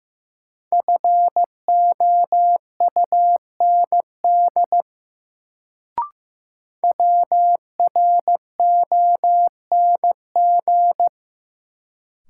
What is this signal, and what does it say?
Morse code, heard as a keyed tone, about 15 wpm, 700 Hz. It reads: FOUND WRONG
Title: Sets of 5 Words Encore - New Word Drill - 15wpm